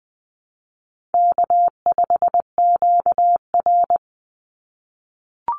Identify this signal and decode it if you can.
Morse code, heard as a keyed tone, about 20 wpm, 700 Hz. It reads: K5QR
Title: NA SOTA Chaser Callsigns 20wpm